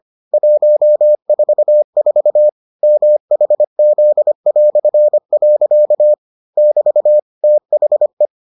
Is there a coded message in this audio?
Yes: Morse code from 0.3 to 8.3 s, heard as a keyed tone, about 25 wpm, 600 Hz. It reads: 144 MHZ". <BT> THE